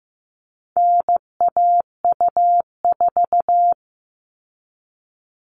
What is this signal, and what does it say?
Morse code, heard as a keyed tone, about 15 wpm, 700 Hz. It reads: NAU4
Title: All Cut Numbers - 4-Digit 15wpm